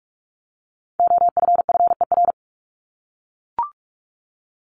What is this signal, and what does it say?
Morse code, heard as a keyed tone, about 44 wpm, 700 Hz. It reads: OFFER
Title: ED Suffix - Top 100 Words with Base Words 45wpm